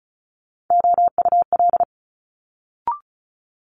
Morse code, heard as a keyed tone, about 36 wpm, 700 Hz. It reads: OUL
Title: Multi-letter Phonemes Quick Fire 35wpm